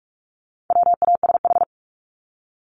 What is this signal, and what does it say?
Morse code, heard as a keyed tone, about 44 wpm, 700 Hz. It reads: WASH
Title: Top 1751-2000 Words 45wpm